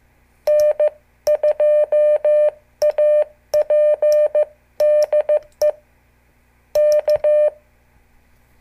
Morse code, heard as a keyed tone, about 14 wpm, 600 Hz. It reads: N2AP DE K
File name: morse / qso / CQ